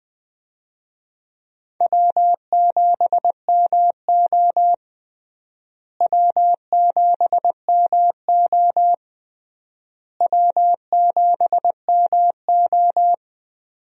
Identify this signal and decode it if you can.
Morse code, heard as a keyed tone, about 20 wpm, 700 Hz. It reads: W7MO W7MO W7MO
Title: USA Call Signs - 4 Characters - Repeated 3x - 20wpm